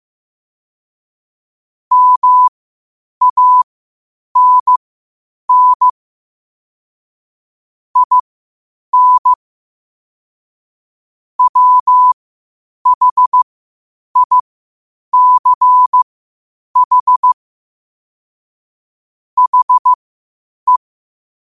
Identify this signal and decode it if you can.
Morse code, heard as a keyed tone, about 15 wpm, spaced as for 8 wpm, 1000 Hz. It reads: MANN IN WHICH HE
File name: qotdfiles / 8wpm